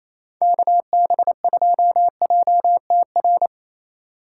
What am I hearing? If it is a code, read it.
Morse code, heard as a keyed tone, about 34 wpm, 700 Hz. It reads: KB2JTR
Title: Speed Racing - USA Call Signs 28wpm